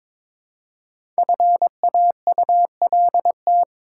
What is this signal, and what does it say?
Morse code, heard as a keyed tone, about 22 wpm, 700 Hz. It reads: FAULT
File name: 5 Letter Words - First 3 Letters Given - 22wpm